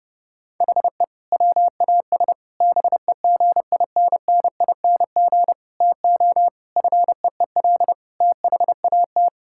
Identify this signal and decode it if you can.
Morse code, heard as a keyed tone, about 30 wpm, 700 Hz. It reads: HE WAS BEGINNING TO FEEL THAT
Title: Sentences from Top 1000 Words - Review Part 5 - 30wpm